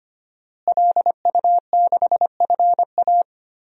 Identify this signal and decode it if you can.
Morse code, heard as a keyed tone, about 25 wpm, 700 Hz. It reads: LU6FA
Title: DXCC-100 Call Signs - 5 Characters - Repeated 2x - 25wpm